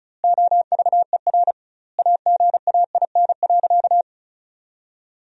Morse code, heard as a keyed tone, about 34 wpm, 700 Hz. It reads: OVER AGAIN.
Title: 2x Word-Spacing - Sentences from Top 500 Words - 35wpm